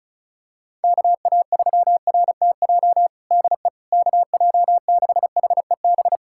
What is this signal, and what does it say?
Morse code, heard as a keyed tone, about 36 wpm, 700 Hz. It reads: KA3RTJ DE KJ6HEB
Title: USA Callsign DE Callsign 35wpm